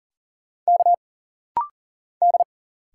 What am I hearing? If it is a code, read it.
Morse code, heard as a keyed tone, about 40 wpm, 700 Hz. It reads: K D
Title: Single Letter-Number - Rapid-Fire 40wpm